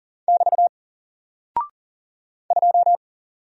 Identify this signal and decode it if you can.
Morse code, heard as a keyed tone, about 40 wpm, 700 Hz. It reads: <BT> 2
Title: Single Character (Letter:Number:Punctuation:Prosign) Alt 40wpm